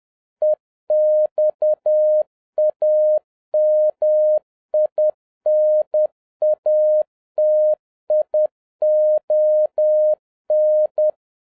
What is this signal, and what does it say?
Morse code, heard as a keyed tone, about 10 wpm, 610 Hz. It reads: EXAMINATION